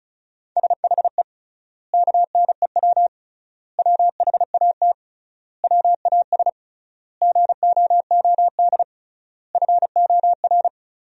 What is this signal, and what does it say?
Morse code, heard as a keyed tone, about 34 wpm, 700 Hz. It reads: SHE KNEW WHAT WAS GOOD FOR
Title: 3x Word-Spacing - Sentences from Top 400 Words - 35wpm